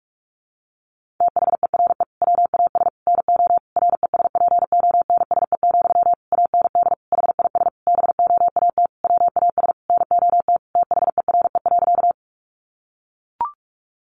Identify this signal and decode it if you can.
Morse code, heard as a keyed tone, about 44 wpm, 700 Hz. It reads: THERE WAS NO RESPONSE, AND HIS BOAT WAS NOT THERE.
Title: Sentences from Top 1400 Words 45wpm